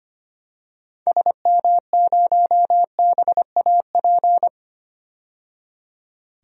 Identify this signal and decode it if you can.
Morse code, heard as a keyed tone, about 25 wpm, 700 Hz. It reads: SM0BAP
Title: DXCC-100 Call Signs - 6 Characters - 25wpm